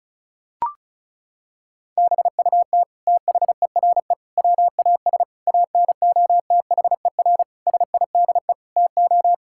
Morse code, heard as a keyed tone, about 36 wpm, 700 Hz. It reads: BUT THERE WAS ANOTHER SIDE TO
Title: Sentences from Top 1600 Words 35wpm